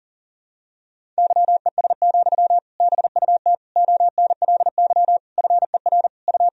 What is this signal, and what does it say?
Morse code, heard as a keyed tone, about 40 wpm, 700 Hz. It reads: YES, BUT ONLY FER U
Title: Sentences from Top 500 Words and Abbreviations 40wpm